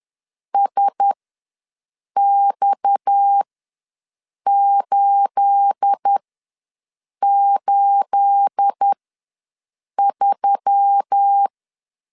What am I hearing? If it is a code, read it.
Morse code, heard as a keyed tone, about 10 wpm, spaced as for 6 wpm, 800 Hz. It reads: SX883
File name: LEZIONI / 9 28 ws 10wpm 4124